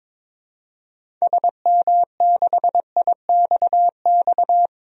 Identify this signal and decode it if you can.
Morse code, heard as a keyed tone, about 21 wpm, 700 Hz. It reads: SM6IXX
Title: DXCC-100 Call Signs - 6 Characters - 22wpm